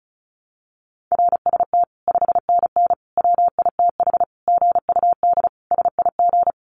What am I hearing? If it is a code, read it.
Morse code, heard as a keyed tone, about 34 wpm, 700 Hz. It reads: RST 5NN WITH GUD SIG
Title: Common QSO Phrases 35wpm